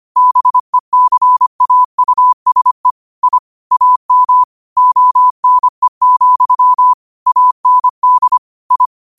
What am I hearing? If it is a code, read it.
Morse code, heard as a keyed tone, about 25 wpm, 1000 Hz. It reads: BECAUSE I AM ONE, AND I
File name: qotdfiles / 25wpm